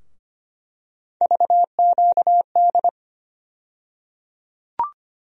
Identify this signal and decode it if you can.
Morse code, heard as a keyed tone, about 25 wpm, 700 Hz. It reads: VQD
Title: USA Call Sign Postfixes - 3 Letters 25wpm